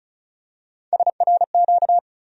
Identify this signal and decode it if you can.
Morse code, heard as a keyed tone, about 36 wpm, 700 Hz. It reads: SRQ
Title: USA Call Sign Postfixes - 3 Letters 35wpm